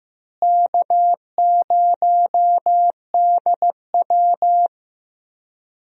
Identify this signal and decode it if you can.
Morse code, heard as a keyed tone, about 15 wpm, 700 Hz. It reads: K0DW
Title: USA Call Signs - 4 Characters - Repeated 3x - 15wpm